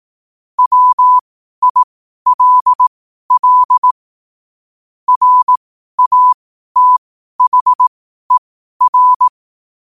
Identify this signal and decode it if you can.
Morse code, heard as a keyed tone, about 18 wpm, spaced as for 12 wpm, 1000 Hz. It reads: WILL RATHER
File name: 12wpm